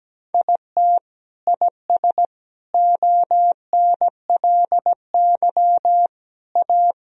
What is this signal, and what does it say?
Morse code, heard as a keyed tone, about 17 wpm, 700 Hz. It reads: IT IS ONLY A
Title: Sentences from Top 1000 Words - Review Part 2 - 17wpm